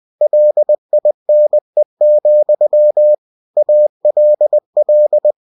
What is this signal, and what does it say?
Morse code, heard as a keyed tone, about 20 wpm, 600 Hz. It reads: LINE, ALL
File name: cw-books / winnie-the-pooh / winnie-20wpm-page-0079